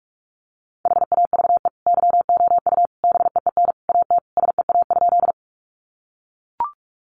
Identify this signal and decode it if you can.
Morse code, heard as a keyed tone, about 44 wpm, 700 Hz. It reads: HAVE YOU BEEN AT SEA?
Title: Sentences from Top 300 Words 45wpm